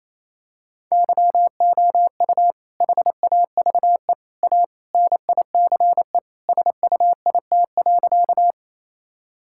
Morse code, heard as a keyed tone, about 28 wpm, 700 Hz. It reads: YOU HAVE A NICE SUIT.
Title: Sentences from Top 1600 Words 28wpm